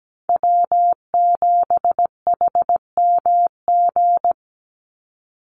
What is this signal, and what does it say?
Morse code, heard as a keyed tone, about 17 wpm, 700 Hz. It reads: W7HMG
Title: USA Call Signs 17wpm